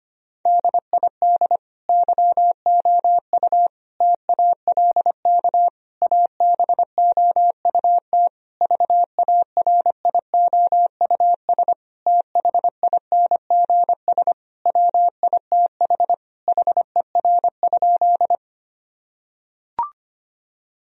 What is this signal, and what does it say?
Morse code, heard as a keyed tone, about 25 wpm, 700 Hz. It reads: DID YOU TALK ABOUT VARIOUS THINGS WITH HER?
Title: Sentences from Top 1000 Words - Review Part 4 - 25wpm